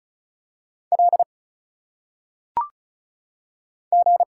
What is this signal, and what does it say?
Morse code, heard as a keyed tone, about 34 wpm, 700 Hz. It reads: L G
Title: Single Letters - Phonetic Pronunciation 35wpm